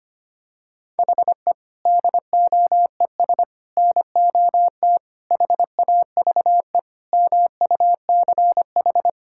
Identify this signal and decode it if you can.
Morse code, heard as a keyed tone, about 25 wpm, 700 Hz. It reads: HE DOES NOT HAVE MUCH
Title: Sentences from Top 1000 Words - Review Part 5 - 25wpm